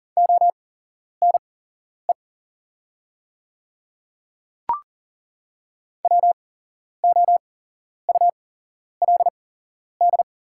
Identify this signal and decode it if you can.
Morse code, heard as a keyed tone, about 40 wpm, spaced as for 11 wpm, 700 Hz. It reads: ONE WOULD
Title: 8x Character Spacing - Top 500 Words - 40wpm